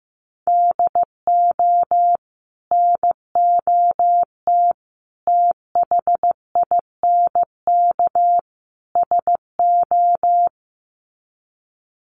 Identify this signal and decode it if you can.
Morse code, heard as a keyed tone, about 15 wpm, 700 Hz. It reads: DO NOT THINK SO
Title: Sets of 4 Words 15wpm